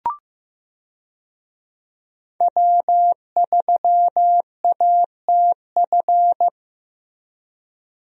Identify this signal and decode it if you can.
Morse code, heard as a keyed tone, about 16 wpm, 700 Hz. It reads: W3ATF
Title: USA Call Signs - 5 Characters - Repeated 2x - 15wpm